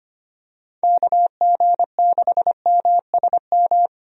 Morse code, heard as a keyed tone, about 25 wpm, 700 Hz. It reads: KG6MSM
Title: USA Call Signs - 6 Characters - Repeated 3x - 25wpm